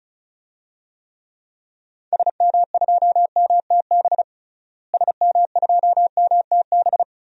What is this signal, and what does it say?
Morse code, heard as a keyed tone, about 36 wpm, 700 Hz. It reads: SM2MTB SM2MTB
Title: DXCC-100 Call Signs - 6 Characters - Repeated 2x - 35wpm